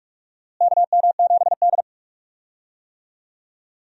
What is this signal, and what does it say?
Morse code, heard as a keyed tone, about 48 wpm, 700 Hz. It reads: KM7D
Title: USA Call Signs - 4 Characters - Repeated 2x - 45wpm